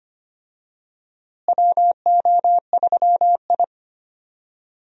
Morse code, heard as a keyed tone, about 25 wpm, 700 Hz. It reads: WO3I
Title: USA Call Signs - 4 Characters - Repeated 3x - 25wpm